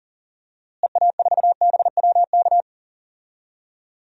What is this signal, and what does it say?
Morse code, heard as a keyed tone, about 39 wpm, 700 Hz. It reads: EA4BWK
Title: DXCC-100 Call Signs - 6 Characters - Repeated 2x - 40wpm